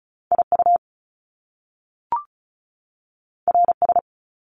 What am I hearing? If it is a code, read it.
Morse code, heard as a keyed tone, about 36 wpm, 700 Hz. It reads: IU RS
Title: Top Two Letter Combinations in Words 35wpm